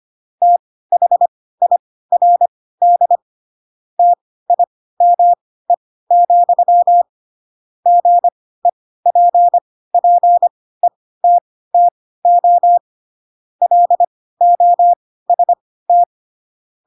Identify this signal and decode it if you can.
Morse code, heard as a keyed tone, about 25 wpm, spaced as for 16 wpm, 700 Hz. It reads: THIRD TIME, GEPPETTO LOST
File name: Pinocchio / Pinocchio ch2 10 25 ik1hge